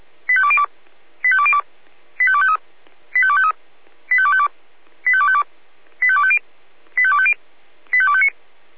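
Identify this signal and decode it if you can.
Five-tone selective call (ZVEI), heard as quick runs of five tones, tones of about 70 ms. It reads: 85202 85202 85253 85253 85252 85252 85250 85250 85259
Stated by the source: ZVEI five-tone selective call 85102, received with an SDR.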